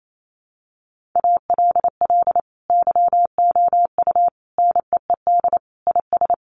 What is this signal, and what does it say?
Morse code, heard as a keyed tone, about 28 wpm, 700 Hz. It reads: ALL YOU NEED IS
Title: Sentences from Top 1500 Words 28wpm